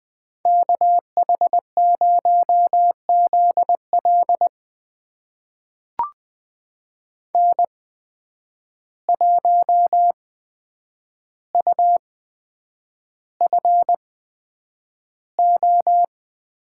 Morse code, heard as a keyed tone, about 20 wpm, spaced as for 9 wpm, 700 Hz. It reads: KH0ZL N1UFO
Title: USA Call Signs - 8x Character Spacing - 20wpm